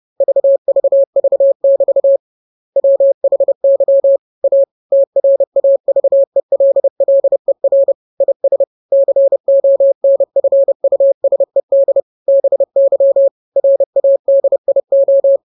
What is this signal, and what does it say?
Morse code, heard as a keyed tone, about 30 wpm, 550 Hz. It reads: VVV<BT> WHY A TRAVELLER IS CONFUSED BY RADIO